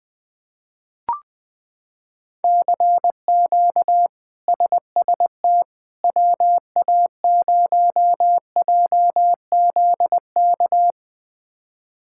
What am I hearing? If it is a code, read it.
Morse code, heard as a keyed tone, about 19 wpm, 700 Hz. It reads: CQ SST WA0JZK
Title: K1USN SST QSO Practice - 20wpm